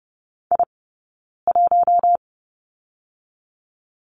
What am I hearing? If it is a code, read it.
Morse code, heard as a keyed tone, about 30 wpm, 700 Hz. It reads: I 1
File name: Lesson 16d - Letter c - Two Characters 30wpm